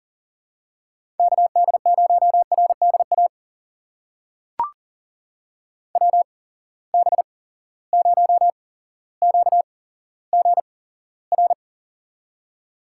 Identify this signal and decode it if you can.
Morse code, heard as a keyed tone, about 40 wpm, spaced as for 17 wpm, 700 Hz. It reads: KD0RDA WB0QGR
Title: USA Call Signs - 6 Characters - 8x Character Spacing - 40wpm